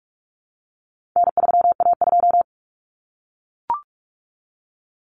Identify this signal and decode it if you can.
Morse code, heard as a keyed tone, about 44 wpm, 700 Hz. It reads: N3A2
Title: Common Cut Numbers - 4-Digit 45wpm